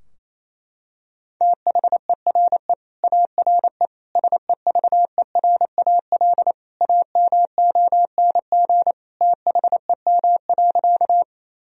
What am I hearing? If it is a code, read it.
Morse code, heard as a keyed tone, about 28 wpm, 700 Hz. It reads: THERE ARE SEVERAL AMONG THEM.
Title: Sentences from Top 1000 Words - Review Part 2 - 28wpm